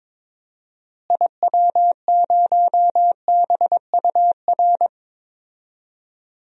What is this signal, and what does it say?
Morse code, heard as a keyed tone, about 22 wpm, 700 Hz. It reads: IW0BUR
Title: DXCC-100 Call Signs - 6 Characters - Repeated 3x - 22wpm